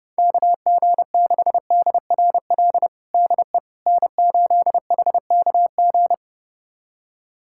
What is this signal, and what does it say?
Morse code, heard as a keyed tone, about 30 wpm, 700 Hz. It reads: KG6DRL DE N8HKG
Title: USA Callsign DE Callsign 30wpm